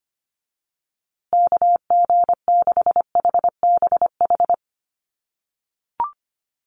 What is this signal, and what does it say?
Morse code, heard as a keyed tone, about 25 wpm, 700 Hz. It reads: KG6HBH
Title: USA Call Signs Repeated 2x 25wpm